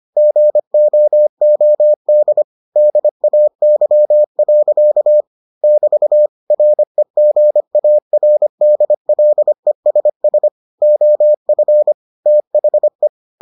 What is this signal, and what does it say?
Morse code, heard as a keyed tone, about 25 wpm, 600 Hz. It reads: GOOD DAY. <BT> REGARDLESS OF THE